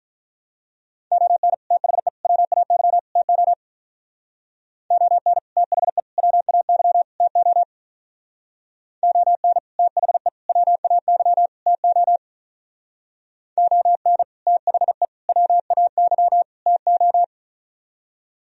Morse code, heard as a keyed tone, about 42 wpm, 700 Hz. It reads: ON THE WAY TO ON THE WAY TO ON THE WAY TO ON THE WAY TO
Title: Speed-Racing - Sets of 4 Words 35wpm